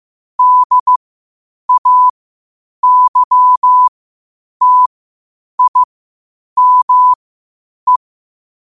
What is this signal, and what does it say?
Morse code, heard as a keyed tone, about 15 wpm, spaced as for 8 wpm, 1000 Hz. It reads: DAYTIME
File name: qotdfiles / 8wpm